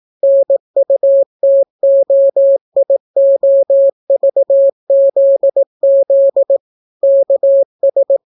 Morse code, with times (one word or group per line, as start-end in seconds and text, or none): 0.2-6.6 NUTOIOVZZ
7.0-8.2 KS